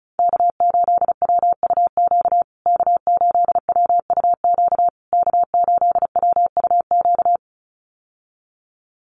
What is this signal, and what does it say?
Morse code, heard as a keyed tone, about 34 wpm, 700 Hz. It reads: K8WUQ K8WUQ K8WUQ
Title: USA Call Signs Repeated 3x 35wpm